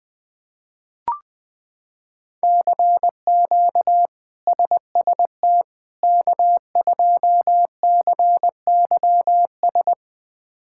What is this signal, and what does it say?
Morse code, heard as a keyed tone, about 20 wpm, 700 Hz. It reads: CQ SST K2CYS
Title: K1USN SST QSO Practice - 20wpm